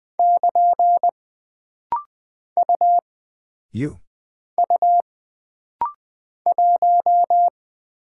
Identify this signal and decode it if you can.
Morse code, heard as a keyed tone, about 20 wpm, 700 Hz. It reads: <KN> U U 1
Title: Single Character (Letter:Number:Punctuation:Prosign) - Rapid-Fire - 20wpm